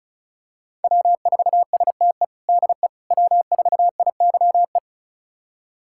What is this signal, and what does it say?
Morse code, heard as a keyed tone, about 36 wpm, 690 Hz. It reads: W4STE DE W4IYE
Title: USA Callsign DE Callsign - 5 Characters - 35wpm